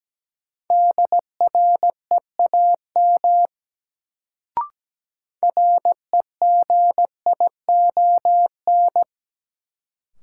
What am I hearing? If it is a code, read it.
Morse code, heard as a keyed tone, about 17 wpm, 700 Hz. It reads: DREAM REGION
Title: Sentences from Top 1500 Words - New Word Drill - 17wpm